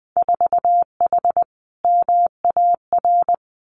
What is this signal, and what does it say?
Morse code, heard as a keyed tone, about 20 wpm, 700 Hz. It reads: 4H MAR
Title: Winter Field Day Exchanges Encore 20wpm